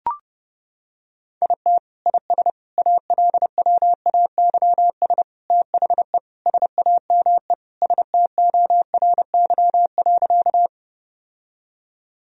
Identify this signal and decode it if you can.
Morse code, heard as a keyed tone, about 30 wpm, 700 Hz. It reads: IT IS ALWAYS THE SAME STORY.
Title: Sentences from Top 1000 Words - Review Part 1 - 30wpm